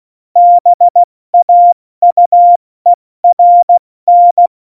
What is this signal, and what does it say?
Morse code, heard as a keyed tone, about 16 wpm, spaced as for 14 wpm, 700 Hz. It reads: BAUERN
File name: Morsekurs / Witze / Witz 21 Juli 2020, 70 F80